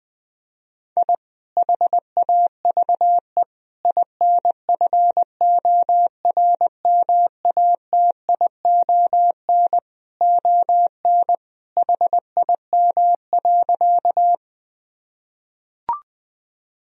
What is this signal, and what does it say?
Morse code, heard as a keyed tone, about 21 wpm, 700 Hz. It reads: I HAVE INFORMATION ON HIM.
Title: Sentences from Top 1000 Words - Review Part 3 - 20wpm